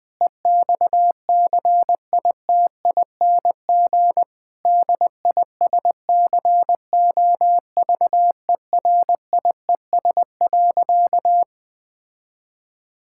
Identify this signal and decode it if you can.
Morse code, heard as a keyed tone, about 20 wpm, 700 Hz. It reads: EXCITING DISCOVERIES.
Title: Difficult Sentences with Hints - 20wpm